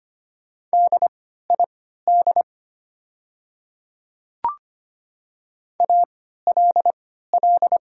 Morse code, heard as a keyed tone, about 25 wpm, spaced as for 14 wpm, 700 Hz. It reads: DID ALL
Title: 3x Character Spacing - Top 100 Words - 25wpm